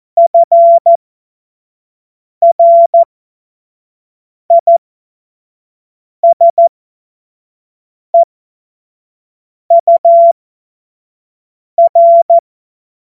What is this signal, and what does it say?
Morse code, heard as a keyed tone, about 14 wpm, spaced as for 5 wpm, 680 Hz. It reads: FRISEUR
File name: Morsekurs / Witze / Witz 21. März 2023 25 F70